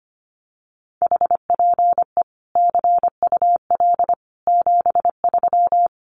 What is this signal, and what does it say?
Morse code, heard as a keyed tone, about 25 wpm, 700 Hz. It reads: HPE CUL 73
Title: Common QSO Phrases 25wpm